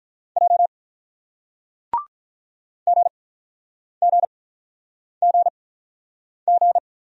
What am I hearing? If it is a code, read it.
Morse code, heard as a keyed tone, about 48 wpm, spaced as for 9 wpm, 700 Hz. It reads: J GGGG